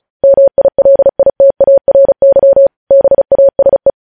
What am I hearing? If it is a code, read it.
Morse code, heard as a keyed tone, about 36 wpm, 550 Hz. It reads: MILITARY BASE
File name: podcast / News headlines-1x-35x35